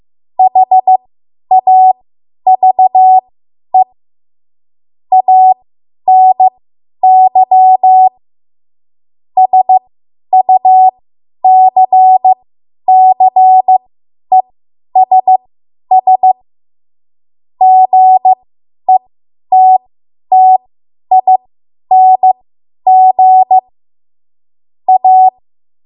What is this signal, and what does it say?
Morse code, heard as a keyed tone, about 15 wpm, spaced as for 10 wpm, 750 Hz. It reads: HAVE ANY SUCCESS GETTING A
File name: Archive / 10 WPM / 10WPM